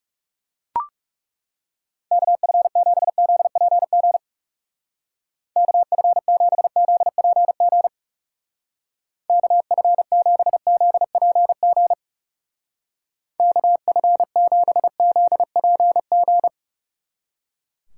Morse code, heard as a keyed tone, about 38 wpm, 700 Hz. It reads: KF7ZPG KF7ZPG KF7ZPG KF7ZPG